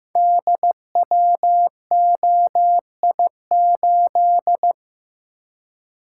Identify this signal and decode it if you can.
Morse code, heard as a keyed tone, about 15 wpm, 700 Hz. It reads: DWOI8
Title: Code Groups - 5 Letter:Numbers - 15wpm